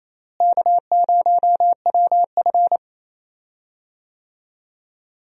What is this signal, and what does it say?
Morse code, heard as a keyed tone, about 28 wpm, 700 Hz. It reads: K0WF
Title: USA Call Signs - 4 Characters - Repeated 3x - 28wpm